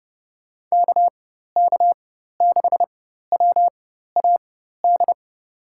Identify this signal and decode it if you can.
Morse code, heard as a keyed tone, about 32 wpm, spaced as for 14 wpm, 700 Hz. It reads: KK6WAD
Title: USA Call Signs - 6 Characters - 4x Character Spacing - 30wpm